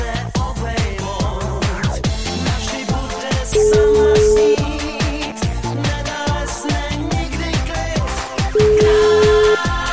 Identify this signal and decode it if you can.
Ringback tone (Europe, 425 Hz), with noise.